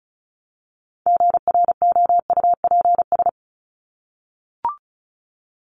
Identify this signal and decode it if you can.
Morse code, heard as a keyed tone, about 34 wpm, 700 Hz. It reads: GROUPS
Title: ES:S Suffix - Top 100 Words with Base Words 35wpm